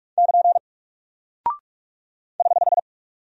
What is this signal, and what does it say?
Morse code, heard as a keyed tone, about 44 wpm, 700 Hz. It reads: <KN> <HH>
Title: Single Character (Letter:Number:Punctuation:Prosign) 45wpm